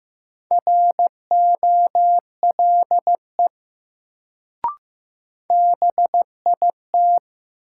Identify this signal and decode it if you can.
Morse code, heard as a keyed tone, about 15 wpm, 700 Hz. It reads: ROLE BIT
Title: Sets of 3 Words - New Word Drill - 15wpm